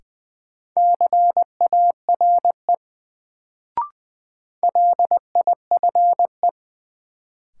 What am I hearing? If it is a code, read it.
Morse code, heard as a keyed tone, about 21 wpm, 700 Hz. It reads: CARE LIFE
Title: Sentences from Top 300 Words - New Word Drill - 20wpm